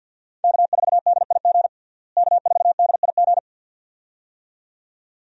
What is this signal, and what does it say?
Morse code, heard as a keyed tone, about 49 wpm, 700 Hz. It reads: K4DIG K4DIG
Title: USA Call Signs - 5 Characters - Repeated 2x - 50wpm